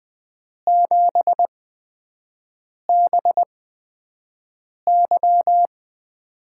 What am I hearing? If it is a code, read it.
Morse code, heard as a keyed tone, about 20 wpm, spaced as for 5 wpm, 700 Hz. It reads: 7BY
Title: USA Call Signs - 5 Characters - 8x Character Spacing - 20wpm